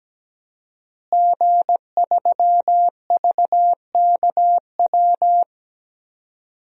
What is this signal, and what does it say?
Morse code, heard as a keyed tone, about 17 wpm, 700 Hz. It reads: G3VKW
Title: DXCC-100 Call Signs - 5 Characters - Repeated 2x - 17wpm